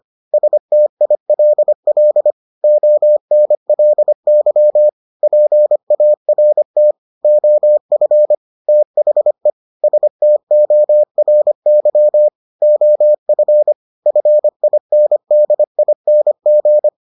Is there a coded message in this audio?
Yes: Morse code from 0.3 to 16.9 s, heard as a keyed tone, about 25 wpm, 600 Hz. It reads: STILL ONLY PART OF THE STORY OF FINDING